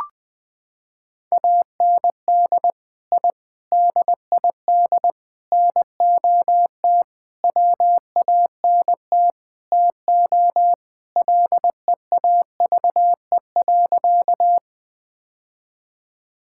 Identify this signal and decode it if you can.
Morse code, heard as a keyed tone, about 20 wpm, 700 Hz. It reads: AND I DID NOT WANT TO LEAVE.
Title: Sentences with 5-Word Phrases - 20wpm